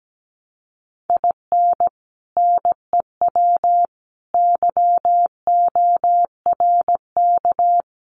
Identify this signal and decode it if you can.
Morse code, heard as a keyed tone, about 17 wpm, 700 Hz. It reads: IN NEW YORK
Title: Sets of 4 Words 17wpm